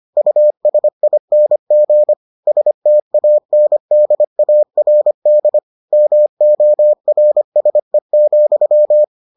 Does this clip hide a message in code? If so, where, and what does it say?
Yes: Morse code from 0.2 to 9.0 s, heard as a keyed tone, about 25 wpm, 600 Hz. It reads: USING STANDARD MORSE,